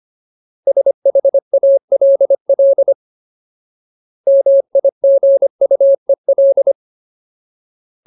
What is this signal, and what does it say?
Morse code, heard as a keyed tone, about 25 wpm, 550 Hz. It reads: SHALL MIGUEL
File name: Ultimate Word List / 25WPM Ultimate